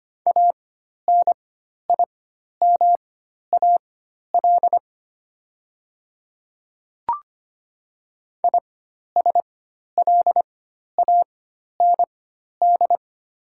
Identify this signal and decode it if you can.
Morse code, heard as a keyed tone, about 25 wpm, spaced as for 12 wpm, 700 Hz. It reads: ANIMAL ISLAND
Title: 4x Character Spacing - Top 500 Words - 25wpm